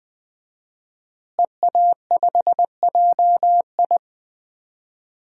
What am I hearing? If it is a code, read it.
Morse code, heard as a keyed tone, about 20 wpm, 700 Hz. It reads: EA5JI
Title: DXCC-100 Call Signs - 5 Characters - Repeated 3x - 20wpm